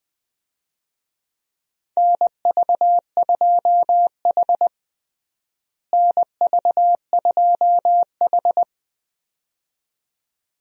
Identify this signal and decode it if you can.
Morse code, heard as a keyed tone, about 21 wpm, 700 Hz. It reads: NV2H NV2H
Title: USA Call Signs - 4 Characters - Repeated 2x - 20wpm